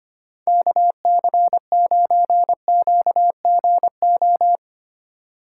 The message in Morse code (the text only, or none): KC9QGO